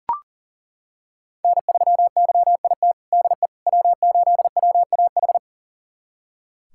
Morse code, heard as a keyed tone, about 40 wpm, 700 Hz. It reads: N3YIT DE W8WAH
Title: USA Callsign DE Callsign - 5 Characters - 40wpm